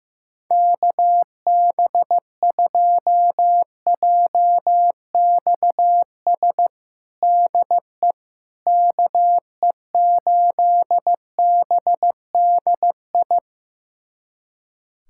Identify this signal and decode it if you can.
Morse code, heard as a keyed tone, about 15 wpm, 700 Hz. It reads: KB2JXS DE KE8BDI
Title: USA Callsign DE Callsign 15wpm